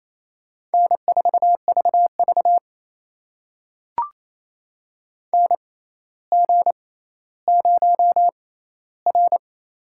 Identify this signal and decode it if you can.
Morse code, heard as a keyed tone, about 28 wpm, spaced as for 14 wpm, 700 Hz. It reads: N4VV NG0R
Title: USA Call Signs - 4 Characters - 6x Character Spacing - 28wpm